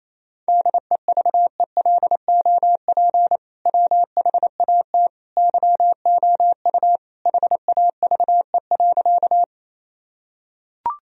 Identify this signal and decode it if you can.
Morse code, heard as a keyed tone, about 28 wpm, 700 Hz. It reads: DEVELOP WHAT YOU HAVE.
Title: Sentences from Top 1000 Words - Review Part 2 - 28wpm